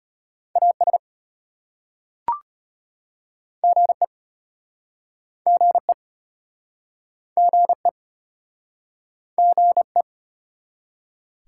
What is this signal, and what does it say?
Morse code, heard as a keyed tone, about 33 wpm, 700 Hz. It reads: AS GE GE GE GE